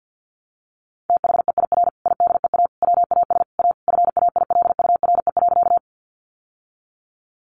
Morse code, heard as a keyed tone, about 50 wpm, 700 Hz. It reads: THEIR IDEA WAS A FAILURE.
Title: Sentences from Top 900 Words 50wpm